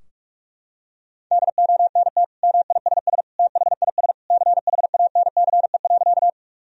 Morse code, heard as a keyed tone, about 41 wpm, 700 Hz. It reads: DONT MISS THIS CHANCE.
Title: Sentences from Top 1000 Words - Review Part 3 - 45wpm